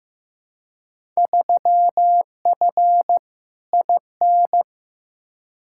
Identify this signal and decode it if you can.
Morse code, heard as a keyed tone, about 15 wpm, 700 Hz. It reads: 3F IN
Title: Field Day Exchanges - 15wpm